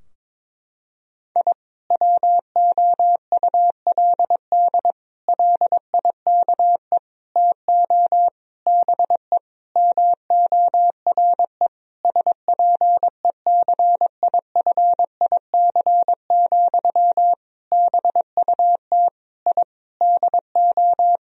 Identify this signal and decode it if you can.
Morse code, heard as a keyed tone, about 22 wpm, 700 Hz. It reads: I WOULD LIKE TO BE MORE SPECIFIC, BUT I DO
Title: Sentences from Top 1000 Words - Review Part 3 - 22wpm